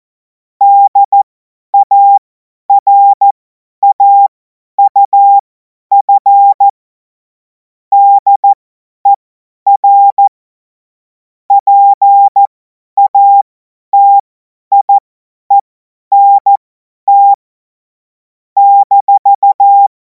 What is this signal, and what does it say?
Morse code, heard as a keyed tone, about 14 wpm, spaced as for 10 wpm, 800 Hz. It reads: DARAUF DER PATIENT -
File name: Morsekurs / Witze / Witz 18. August 2026 50 F70